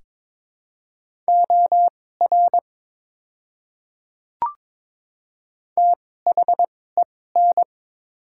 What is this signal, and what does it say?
Morse code, heard as a keyed tone, about 22 wpm, spaced as for 16 wpm, 700 Hz. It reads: OR THEN
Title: 2x Character Spacing - Top 200 Words - 22wpm